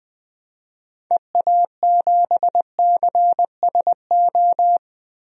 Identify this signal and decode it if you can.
Morse code, heard as a keyed tone, about 20 wpm, 700 Hz. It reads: EA7CSO
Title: DXCC-100 Call Signs - 6 Characters - Repeated 3x - 20wpm